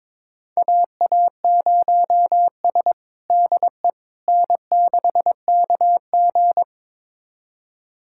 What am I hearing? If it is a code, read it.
Morse code, heard as a keyed tone, about 22 wpm, 700 Hz. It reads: AA0S DE N6KG
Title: USA Callsign DE Callsign - 4 Characters - 22wpm